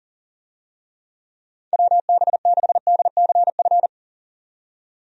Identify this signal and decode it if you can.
Morse code, heard as a keyed tone, about 40 wpm, 700 Hz. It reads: WB6DCF
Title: USA Call Signs - 6 Characters - 40wpm